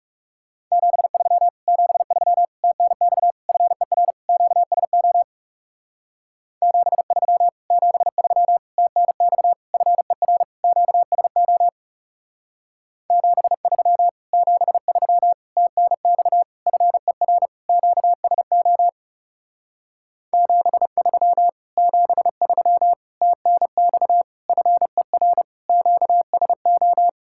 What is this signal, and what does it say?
Morse code, heard as a keyed tone, about 37 wpm, 700 Hz. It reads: 73 73 TNX FER QSO 73 73 TNX FER QSO 73 73 TNX FER QSO 73 73 TNX FER QSO
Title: Speed-Racing Common QSO Phrases Encore - 30wpm